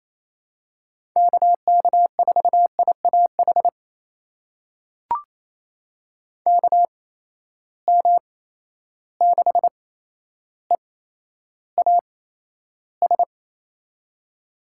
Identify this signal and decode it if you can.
Morse code, heard as a keyed tone, about 28 wpm, spaced as for 12 wpm, 700 Hz. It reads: KK4IAH KM6EAS
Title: USA Call Signs - 6 Characters - 8x Character Spacing - 28wpm